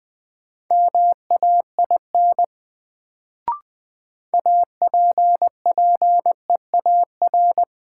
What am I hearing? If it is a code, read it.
Morse code, heard as a keyed tone, about 21 wpm, 700 Hz. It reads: MAIN APPEAR
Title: Sentences from Top 1000 Words - Review Part 2 - New Word Drill - 20wpm